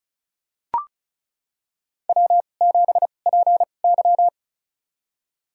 Morse code, heard as a keyed tone, about 36 wpm, spaced as for 25 wpm, 700 Hz. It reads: W7PY
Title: USA Call Signs - 4 Characters - 2x Character Spacing - 35wpm